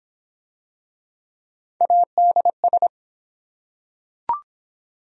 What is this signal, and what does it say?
Morse code, heard as a keyed tone, about 23 wpm, 700 Hz. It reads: ADS